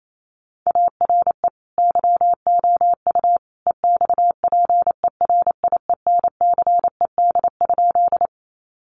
Morse code, heard as a keyed tone, about 28 wpm, 700 Hz. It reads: ARE YOU EXPERIENCED?
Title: Sentences from Top 1100 Words 28wpm